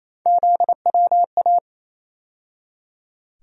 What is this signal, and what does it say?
Morse code, heard as a keyed tone, about 27 wpm, 700 Hz. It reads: ZWA
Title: USA Call Sign Postfixes - 3 Letters 28wpm